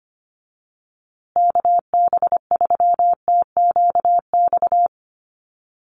Morse code, heard as a keyed tone, about 25 wpm, 700 Hz. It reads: KB3TQX
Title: USA Call Signs Repeated 2x 25wpm